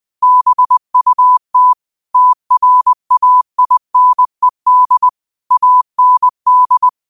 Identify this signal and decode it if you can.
Morse code, heard as a keyed tone, about 20 wpm, 1000 Hz. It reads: BUT TRAINED AND